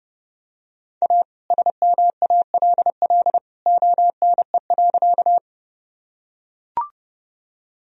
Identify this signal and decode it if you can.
Morse code, heard as a keyed tone, about 30 wpm, 700 Hz. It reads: A SMALL ONE.
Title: Sentences from Top 1000 Words - Review Part 1 - 30wpm